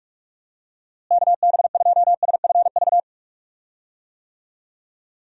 Morse code, heard as a keyed tone, about 44 wpm, 700 Hz. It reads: KB2SFV
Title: USA Call Signs - 6 Characters - Repeated 2x - 45wpm